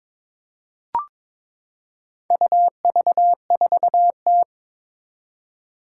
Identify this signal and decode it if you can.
Morse code, heard as a keyed tone, about 22 wpm, 700 Hz. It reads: UV4T
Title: Four Challenging Letter-Number Combinations 22wpm